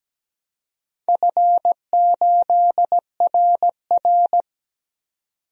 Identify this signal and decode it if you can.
Morse code, heard as a keyed tone, about 17 wpm, 700 Hz. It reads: F8RR
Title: DXCC-100 Call Signs - 4 Characters - 17wpm